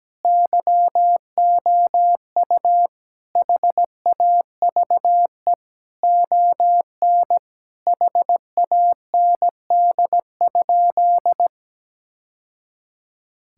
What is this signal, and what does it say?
Morse code, heard as a keyed tone, about 17 wpm, 700 Hz. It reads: YOU HAVE ON HAND?
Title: Sentences from Top 1000 Words - Review Part 5 - 17wpm